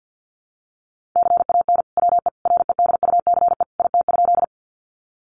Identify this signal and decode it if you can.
Morse code, heard as a keyed tone, about 50 wpm, 700 Hz. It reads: CAN WE REDUCE IT?
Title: Sentences from Top 1100 Words 50wpm